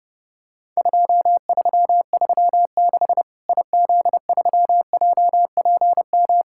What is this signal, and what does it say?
Morse code, heard as a keyed tone, about 30 wpm, 700 Hz. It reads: 2336 IZ3JPM
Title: Worked All Europe DX Contest - QTC Exchange - 30wpm